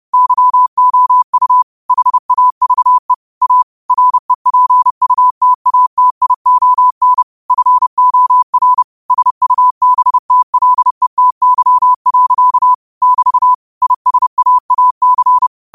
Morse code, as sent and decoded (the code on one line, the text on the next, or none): -.-- --- ..- / .... .- ...- . / .- / .-. .--. ..- - .- - .. --- -. / ..-. --- .-. / ... ..- -... - .-.. - -.-- .-.-.- / -...- / .. ... .- .- -.-.
YOU HAVE A REPUTATION FOR SUBTLETY. <BT> ISAAC